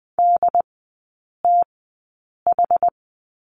Morse code, heard as a keyed tone, about 20 wpm, spaced as for 8 wpm, 690 Hz. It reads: DTH
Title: 081 Lesson 14e - Number 2 - Three Characters 20wpm